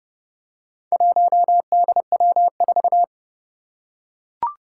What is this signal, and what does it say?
Morse code, heard as a keyed tone, about 30 wpm, 700 Hz. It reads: 1DW4
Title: Four Challenging Letter-Number Combinations 30wpm